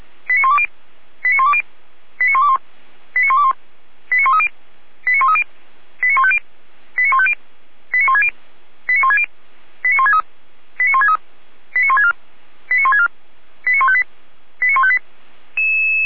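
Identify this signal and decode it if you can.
Five-tone selective call (ZVEI), heard as quick runs of five tones, tones of about 70 ms. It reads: 89120 89120 89121 89121 89130 89130 89150 89150 89160 89160 89163 89163 89164 89164 89158 89158
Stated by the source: ZVEI five-tone call received on SDR.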